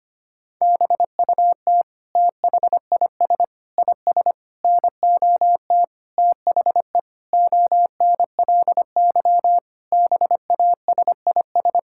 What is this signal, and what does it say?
Morse code, heard as a keyed tone, about 25 wpm, 700 Hz. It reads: BUT THIS IS NOT THE ONLY BASIS
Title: Sentences from Top 1600 Words 25wpm